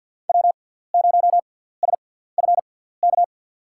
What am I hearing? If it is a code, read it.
Morse code, heard as a keyed tone, about 50 wpm, spaced as for 17 wpm, 700 Hz. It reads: W0SFK
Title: USA Call Signs - 5 Characters - 6x Character Spacing - 50wpm